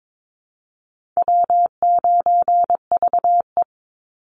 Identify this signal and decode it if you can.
Morse code, heard as a keyed tone, about 22 wpm, 700 Hz. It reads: W9VE
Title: Call Signs - 22wpm